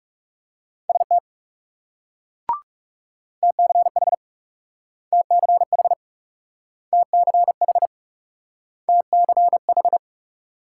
Morse code, heard as a keyed tone, about 38 wpm, 700 Hz. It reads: ST TCH TCH TCH TCH